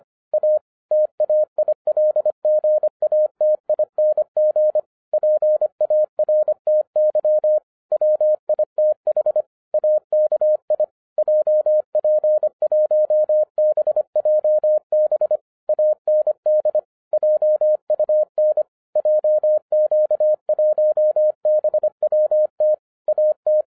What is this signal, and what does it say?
Morse code, heard as a keyed tone, about 25 wpm, 610 Hz. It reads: A TAILGATING PARTY WITH AKI JP1BJB AND JUN JQ1BWT AT